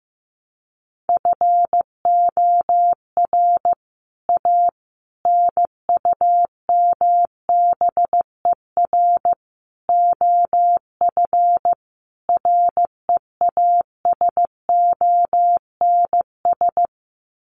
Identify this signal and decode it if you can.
Morse code, heard as a keyed tone, about 15 wpm, 700 Hz. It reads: FOR A NUMBER OF REASONS
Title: Sets of 5 Words Encore2 15wpm